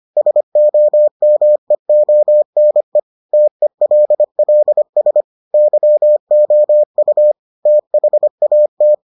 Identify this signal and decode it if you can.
Morse code, heard as a keyed tone, about 25 wpm, 600 Hz. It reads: SOMEONE TELLS YOU THAT